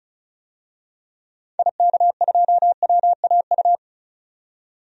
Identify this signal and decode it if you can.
Morse code, heard as a keyed tone, about 36 wpm, 700 Hz. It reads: IK2WAU
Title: DXCC-100 Call Signs - 6 Characters - 35wpm